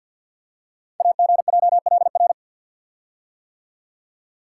Morse code, heard as a keyed tone, about 48 wpm, 690 Hz. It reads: AGJLR
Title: Code Groups - 5 Letters - 50wpm